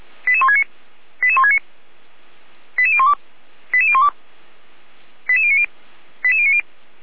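Five-tone selective call (ZVEI), heard as quick runs of five tones, tones of about 70 ms. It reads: 88169 88169 88012 88012 88090 88090
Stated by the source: Received on SDR as a ZVEI five-tone selective call, code 88125.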